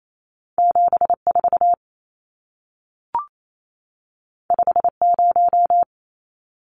Morse code, heard as a keyed tone, about 28 wpm, 700 Hz. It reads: 74 50
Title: Numbers - 2-Digit 28wpm